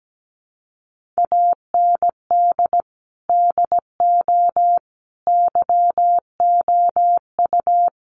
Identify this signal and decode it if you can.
Morse code, heard as a keyed tone, about 17 wpm, 700 Hz. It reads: AND DO YOU
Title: Sentences from Top 1500 Words 17wpm